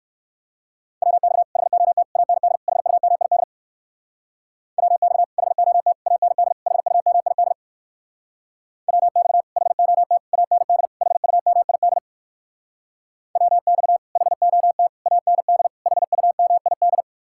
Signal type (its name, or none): Morse code